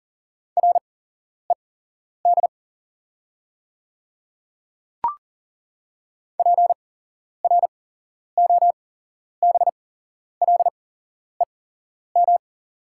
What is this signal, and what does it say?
Morse code, heard as a keyed tone, about 40 wpm, spaced as for 11 wpm, 690 Hz. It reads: RED PROBLEM
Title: 8x Character Spacing - Top 500 Words - 40wpm